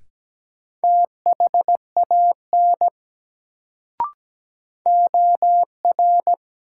Morse code, heard as a keyed tone, about 17 wpm, 700 Hz. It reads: THAN OR
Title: Sentences from Top 100 Words - New Word Drill - 17wpm